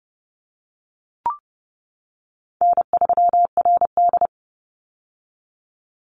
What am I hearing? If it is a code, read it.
Morse code, heard as a keyed tone, about 32 wpm, 700 Hz. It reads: N3RD
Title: Easy Call Signs - 30wpm